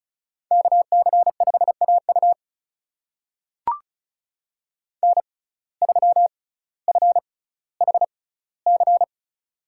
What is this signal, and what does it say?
Morse code, heard as a keyed tone, about 34 wpm, spaced as for 18 wpm, 700 Hz. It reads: KC5AU N3FHC
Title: USA Call Signs - 5 Characters - 6x Character Spacing - 35wpm